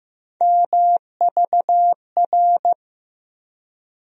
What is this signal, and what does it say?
Morse code, heard as a keyed tone, about 15 wpm, 700 Hz. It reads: MVR
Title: USA Call Sign Postfixes - 3 Letters 15wpm